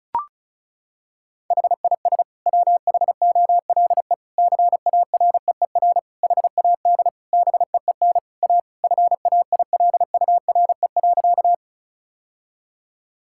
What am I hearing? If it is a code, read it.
Morse code, heard as a keyed tone, about 36 wpm, 700 Hz. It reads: HIS WHOLE CAREER HAD BEEN A FAILURE.
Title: Sentences from Top 1000 Words - Review Part 6 - 35wpm